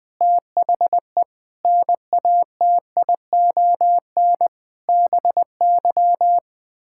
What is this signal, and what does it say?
Morse code, heard as a keyed tone, about 20 wpm, 700 Hz. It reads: THE NATION BY
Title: Sets of 9 Words - 20wpm